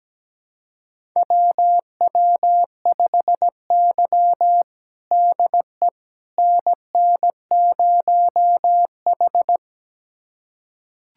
Morse code, heard as a keyed tone, about 17 wpm, 700 Hz. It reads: WW5Y DE NN0H
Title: USA Callsign DE Callsign - 4 Characters - 17wpm